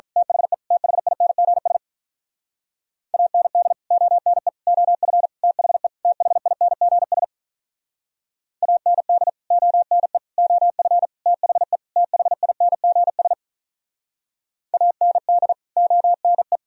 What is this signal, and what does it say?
Morse code, heard as a keyed tone, about 43 wpm, 700 Hz. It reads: THE THINGS AND ONE OF THE THINGS AND ONE OF THE THINGS AND ONE
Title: Speed-Racing - Sets of 5 Words 35wpm